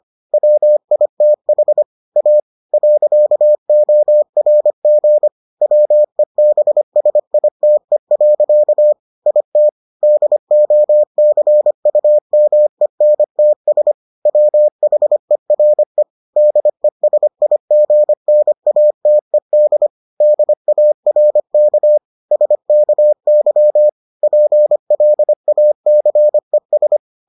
Morse code, heard as a keyed tone, about 25 wpm, 600 Hz. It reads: WITH A .ORG WEBSITE. IT DOCUMENTS WHERE DESIGNATED DARK SKY PLACES